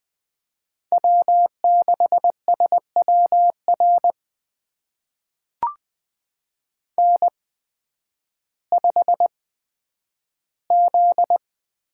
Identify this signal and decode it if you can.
Morse code, heard as a keyed tone, about 20 wpm, spaced as for 11 wpm, 700 Hz. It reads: W6SWR N5Z
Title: USA Call Signs - 5 Characters - 8x Character Spacing - 20wpm